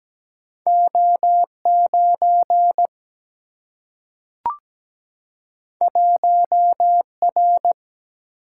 Morse code, heard as a keyed tone, about 17 wpm, 700 Hz. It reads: O9 1R
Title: Single-Letter Single-Number 17wpm